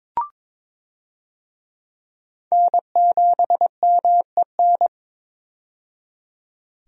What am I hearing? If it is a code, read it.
Morse code, heard as a keyed tone, about 22 wpm, 700 Hz. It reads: N7MEN
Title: USA Call Signs - 5 Characters - 22wpm